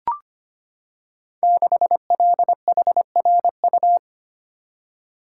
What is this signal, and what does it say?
Morse code, heard as a keyed tone, about 25 wpm, 690 Hz. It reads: 6LHRU
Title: Code Groups - 5 Letter:Numbers - 25wpm